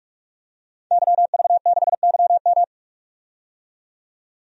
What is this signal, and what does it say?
Morse code, heard as a keyed tone, about 43 wpm, 700 Hz. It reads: YV6YM
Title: DXCC-100 Call Signs - 5 Characters - 45wpm